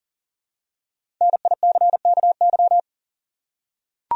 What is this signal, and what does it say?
Morse code, heard as a keyed tone, about 40 wpm, 700 Hz. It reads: NICKY
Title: USA Common Names - 40wpm